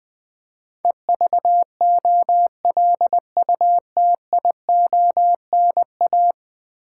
Morse code, heard as a keyed tone, about 20 wpm, 700 Hz. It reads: EVOLUTIONA
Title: Build A Word - 5 Letters At A Time - 12-Letter Words - 20wpm